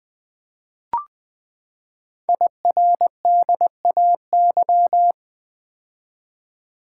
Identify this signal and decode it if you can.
Morse code, heard as a keyed tone, about 20 wpm, 700 Hz. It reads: IRDAY
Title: Code Groups - 5 Letters - 20wpm